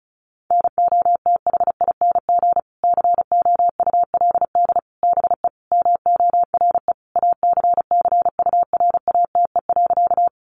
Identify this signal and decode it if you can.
Morse code, heard as a keyed tone, about 34 wpm, 700 Hz. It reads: NOTHING COULD BE MORE ACCURATE.
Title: Sentences from Top 1200 Words 35wpm